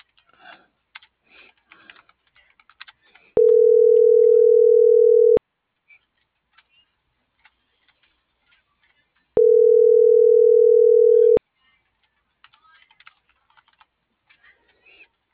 Ringback tone (North America), 440 + 480 Hz.